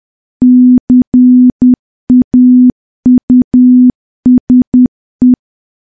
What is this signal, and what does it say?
Morse code, heard as a keyed tone, about 10 wpm, 250 Hz. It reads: CAUSE
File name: morse / morse-training-10WPM